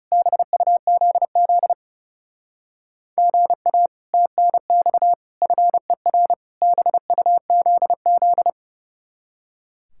Morse code, heard as a keyed tone, about 37 wpm, 700 Hz. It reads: BUZZ GA TNX FER BUZZ
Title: Speed-Racing Common QSO Phrases - 30wpm